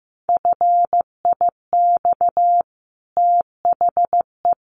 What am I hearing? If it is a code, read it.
Morse code, heard as a keyed tone, about 15 wpm, 700 Hz. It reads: FIX THE
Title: Sentences from Top 1400 Words 15wpm